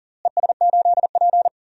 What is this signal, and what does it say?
Morse code, heard as a keyed tone, about 40 wpm, 700 Hz. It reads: ES8P
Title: DXCC-100 Call Signs - 4 Characters - Repeated 3x - 40wpm